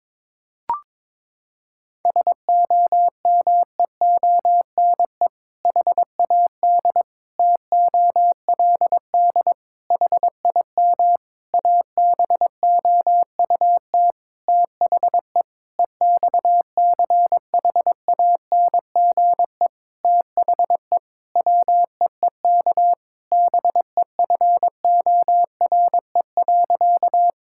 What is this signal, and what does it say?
Morse code, heard as a keyed tone, about 22 wpm, 700 Hz. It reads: SOMEONE HAD TOLD HIM ABOUT THE EXCHANGE THE WEEK BEFORE.
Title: Sentences from Top 1000 Words - Review Part 6 - 22wpm